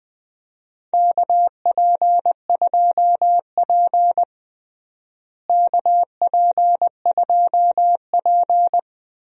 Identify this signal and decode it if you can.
Morse code, heard as a keyed tone, about 21 wpm, 700 Hz. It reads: KP2P KP2P
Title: DXCC-100 Call Signs - 4 Characters - Repeated 2x - 20wpm